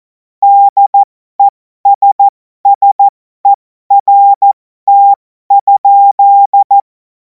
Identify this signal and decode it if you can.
Morse code, heard as a keyed tone, about 14 wpm, spaced as for 12 wpm, 800 Hz. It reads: DESSERT?
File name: Morsekurs / Witze